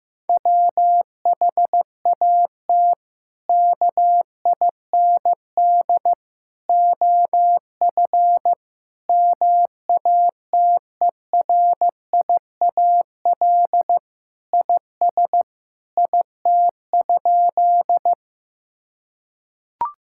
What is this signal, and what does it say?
Morse code, heard as a keyed tone, about 16 wpm, 700 Hz. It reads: WHAT KIND OF MATERIAL IS IT?
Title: Sentences from Top 1000 Words - Review Part 5 - 15wpm